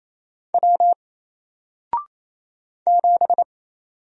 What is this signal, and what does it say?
Morse code, heard as a keyed tone, about 27 wpm, 700 Hz. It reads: W 7
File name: Single-Letter-Number-28wpm